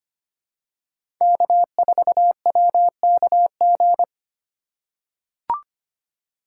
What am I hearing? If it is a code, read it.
Morse code, heard as a keyed tone, about 25 wpm, 700 Hz. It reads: K4WKG
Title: USA Call Signs - 5 Characters - 4x Character Spacing - 25wpm